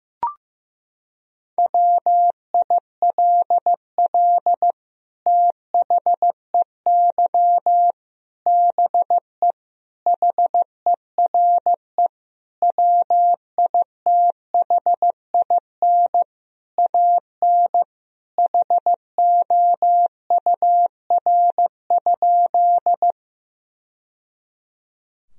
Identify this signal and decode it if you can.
Morse code, heard as a keyed tone, about 15 wpm, 700 Hz. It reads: WILL THEY BE HERE WITHIN AN HOUR?
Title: Sentences from Top 1000 Words - Review Part 4 - 15wpm